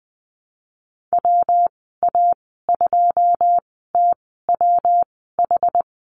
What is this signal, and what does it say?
Morse code, heard as a keyed tone, about 20 wpm, spaced as for 15 wpm, 700 Hz. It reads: WA2TWH
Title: 122 Lesson 20g - Number 3 - Call Signs 20wpm